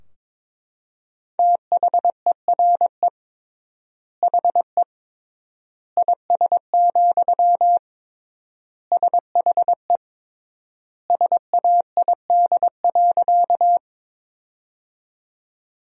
Morse code, heard as a keyed tone, about 22 wpm, 700 Hz. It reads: THERE HE IS, SHE SAID.
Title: 3x Word-Spacing - Sentences from Top 100 Words - 22wpm